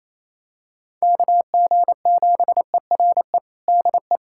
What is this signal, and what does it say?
Morse code, heard as a keyed tone, about 28 wpm, 700 Hz. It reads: KG7ERE DE
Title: USA Callsign DE Callsign 28wpm